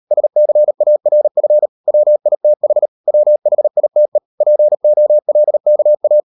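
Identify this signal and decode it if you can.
Morse code, heard as a keyed tone, about 38 wpm, 600 Hz. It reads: SCARF WITH WHITE POLKA
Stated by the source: Morse code practice at 38 WPM, sending Ken Follett's novel Never in English, from CW med Gnister.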